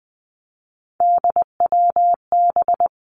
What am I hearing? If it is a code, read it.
Morse code, heard as a keyed tone, about 20 wpm, 700 Hz. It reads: DWB
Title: Three Challenging Letter-Number Combinations 20wpm